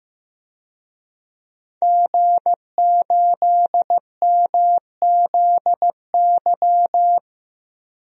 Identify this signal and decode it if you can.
Morse code, heard as a keyed tone, about 15 wpm, 700 Hz. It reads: G8MZY
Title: DXCC-100 Call Signs - 5 Characters - Repeated 2x - 15wpm